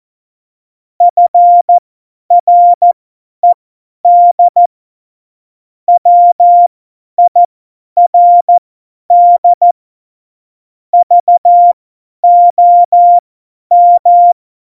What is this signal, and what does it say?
Morse code, heard as a keyed tone, about 14 wpm, spaced as for 10 wpm, 700 Hz. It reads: FRED WIRD VOM
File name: Morsekurs / Witze / Witz 05. Dezember 2017, 50 F70